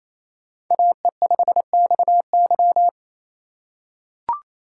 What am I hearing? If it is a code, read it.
Morse code, heard as a keyed tone, about 28 wpm, 700 Hz. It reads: AE5XY
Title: USA Call Signs - 5 Characters - 4x Character Spacing - 28wpm